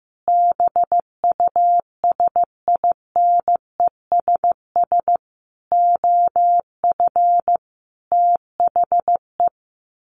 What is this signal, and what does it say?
Morse code, heard as a keyed tone, about 15 wpm, 700 Hz. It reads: BUSINESS OF THE
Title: Sentences from Top 1500 Words 15wpm